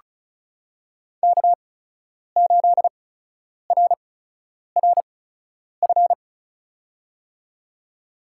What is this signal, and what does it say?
Morse code, heard as a keyed tone, about 36 wpm, spaced as for 10 wpm, 700 Hz. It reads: K8RRF
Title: USA Call Signs - 8x Character Spacing - 35wpm